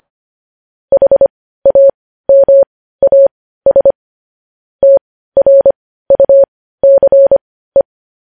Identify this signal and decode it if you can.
Morse code, heard as a keyed tone, about 25 wpm, spaced as for 15 wpm, 550 Hz. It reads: HAMAS TRUCE